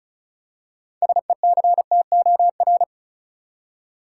Morse code, heard as a keyed tone, about 34 wpm, 700 Hz. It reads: SECTOR
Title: Six Letter Words - 35wpm